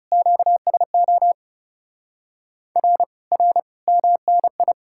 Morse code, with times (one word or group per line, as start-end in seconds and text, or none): 0.1-1.3 QSO
2.8-3.0 R
3.3-3.6 R
3.9-4.7 MNI